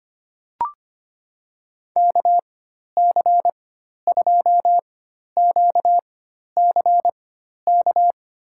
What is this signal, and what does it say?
Morse code, heard as a keyed tone, about 25 wpm, spaced as for 12 wpm, 700 Hz. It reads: KC2QCK
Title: USA Call Signs - 6 Characters - 4x Character Spacing - 25wpm